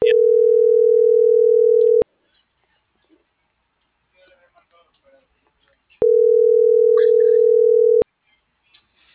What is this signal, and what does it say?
Ringback tone (North America), 440 + 480 Hz.